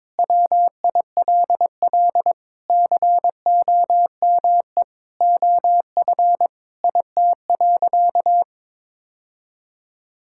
Morse code, heard as a keyed tone, about 22 wpm, 700 Hz. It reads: WILL COME OF IT.
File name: practice / 22wpm / Sentences with 2-Word Phrases - 22wpm